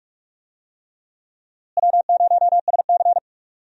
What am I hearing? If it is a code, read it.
Morse code, heard as a keyed tone, about 45 wpm, 700 Hz. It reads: W0SC
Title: USA Call Signs - 4 Characters - Repeated 3x - 45wpm